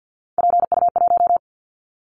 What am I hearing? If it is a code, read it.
Morse code, heard as a keyed tone, about 50 wpm, 700 Hz. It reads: PU1
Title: Callsign Prefixes for Top 50 DXCC - 50wpm